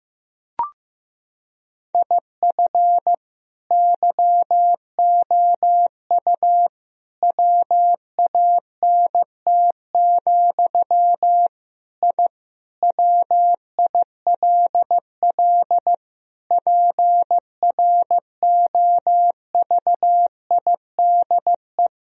Morse code, heard as a keyed tone, about 15 wpm, 700 Hz. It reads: IF YOU WANT, I WILL PROVIDE
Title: Sentences from Top 1000 Words - Review Part 3 - 15wpm